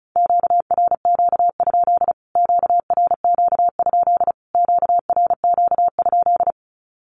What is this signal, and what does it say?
Morse code, heard as a keyed tone, about 34 wpm, 700 Hz. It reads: QRQ? QRQ? QRQ?
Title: Q-Codes Repeated 3x 35wpm